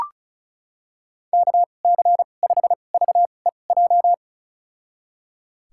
Morse code, heard as a keyed tone, about 36 wpm, spaced as for 26 wpm, 690 Hz. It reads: KC5VEJ
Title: USA Call Signs - 6 Characters - 2x Character Spacing - 35wpm